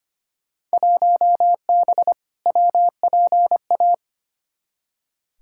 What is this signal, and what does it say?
Morse code, heard as a keyed tone, about 25 wpm, 700 Hz. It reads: 1B WPA
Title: Field Day Exchanges Encore - 25wpm